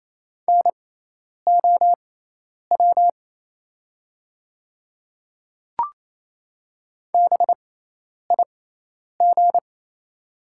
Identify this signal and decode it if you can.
Morse code, heard as a keyed tone, about 28 wpm, spaced as for 10 wpm, 700 Hz. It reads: NOW BIG
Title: 6x Character Spacing - Top 200 Words - 28wpm